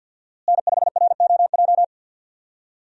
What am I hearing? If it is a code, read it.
Morse code, heard as a keyed tone, about 52 wpm, 700 Hz. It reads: N5ROJ
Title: USA Call Signs - 5 Characters - Repeated 2x - 50wpm